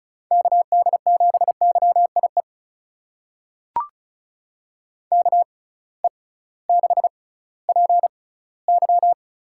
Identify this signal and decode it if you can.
Morse code, heard as a keyed tone, about 36 wpm, spaced as for 19 wpm, 700 Hz. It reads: KD7YIE KE6PY
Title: USA Call Signs - 6 Characters - 6x Character Spacing - 35wpm